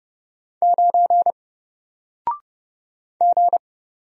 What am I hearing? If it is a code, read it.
Morse code, heard as a keyed tone, about 30 wpm, 700 Hz. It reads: 9 G
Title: Single Letter-Number Alt 30wpm